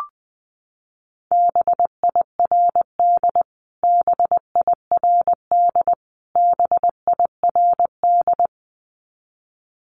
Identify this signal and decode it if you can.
Morse code, heard as a keyed tone, about 20 wpm, 700 Hz. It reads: BIRD BIRD BIRD
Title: Top 500 Words Repeated 20wpm